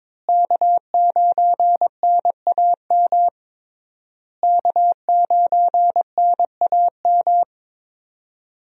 Morse code, heard as a keyed tone, about 22 wpm, 700 Hz. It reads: K9NAM K9NAM
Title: USA Call Signs - 5 Characters - Repeated 3x - 22wpm